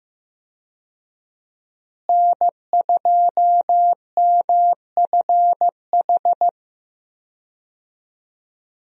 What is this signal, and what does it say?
Morse code, heard as a keyed tone, about 15 wpm, 700 Hz. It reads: N2MFH
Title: USA Call Signs - 5 Characters - Repeated 2x - 15wpm